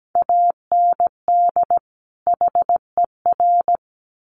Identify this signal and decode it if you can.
Morse code, heard as a keyed tone, about 17 wpm, 700 Hz. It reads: AND HER
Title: Sets of 2 Words 17wpm